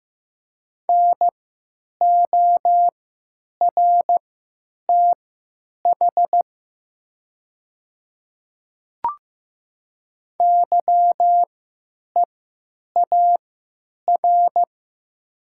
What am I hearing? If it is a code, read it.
Morse code, heard as a keyed tone, about 15 wpm, spaced as for 9 wpm, 700 Hz. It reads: NORTH YEAR
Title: 3x Character Spacing - Top 300 Words - 15wpm